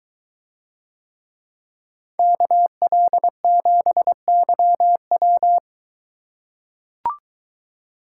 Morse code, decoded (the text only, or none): KL7YW